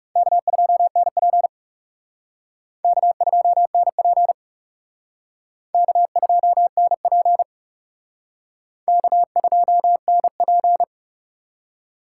Morse code, heard as a keyed tone, about 37 wpm, 700 Hz. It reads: K2NP K2NP K2NP K2NP
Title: Speed Racing - USA Call Signs 30wpm